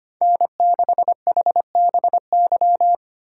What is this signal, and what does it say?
Morse code, heard as a keyed tone, about 25 wpm, 700 Hz. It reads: N6HBY
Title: USA Call Signs - 5 Characters - 4x Character Spacing - 25wpm